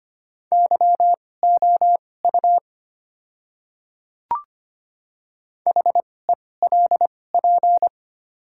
Morse code, heard as a keyed tone, about 25 wpm, spaced as for 18 wpm, 690 Hz. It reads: YOU HELP
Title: 2x Character Spacing - Top 200 Words - 25wpm